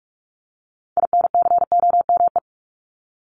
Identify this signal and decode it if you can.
Morse code, heard as a keyed tone, about 44 wpm, 700 Hz. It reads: INCOME
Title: Top 751-1000 Words 45wpm